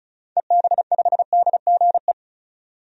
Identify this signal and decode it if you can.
Morse code, heard as a keyed tone, about 35 wpm, 700 Hz. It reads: EB5DGE
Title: DXCC-100 Call Signs - 6 Characters - 35wpm